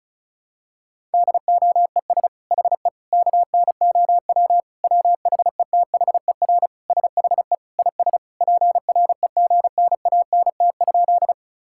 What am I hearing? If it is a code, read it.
Morse code, heard as a keyed tone, about 34 wpm, 700 Hz. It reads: DOES HE KNOW WHETHER SHE IS PREGNANT?
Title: Sentences from Top 1000 Words - Review Part 6 - 35wpm